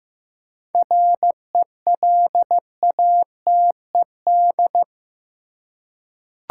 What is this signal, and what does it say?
Morse code, heard as a keyed tone, about 15 wpm, 700 Hz. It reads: RELATED
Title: Seven Letter Words - 15wpm